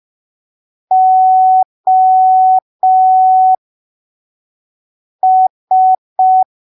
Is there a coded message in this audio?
Yes: Morse code from 0.9 to 6.4 s, heard as a keyed tone, about 5 wpm, spaced as for 3 wpm, 750 Hz. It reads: OS